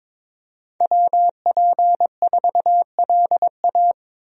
Morse code, heard as a keyed tone, about 22 wpm, 700 Hz. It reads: WP4LA
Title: USA Call Signs - 5 Characters - Repeated 3x - 22wpm